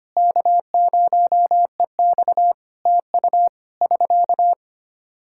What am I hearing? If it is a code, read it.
Morse code, heard as a keyed tone, about 25 wpm, 700 Hz. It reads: K0EX TU <SK>
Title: Practice QSO - Exchange 4 - Ending - 25wpm